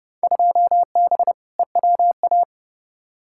Morse code, heard as a keyed tone, about 30 wpm, 700 Hz. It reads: 2B EWA
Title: Field Day Exchanges - 30wpm